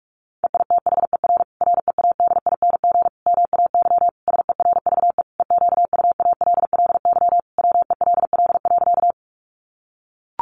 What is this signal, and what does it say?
Morse code, heard as a keyed tone, about 44 wpm, 700 Hz. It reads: EITHER READING MAY SERVE EQUALLY WELL.
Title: Sentences from Top 1400 Words 45wpm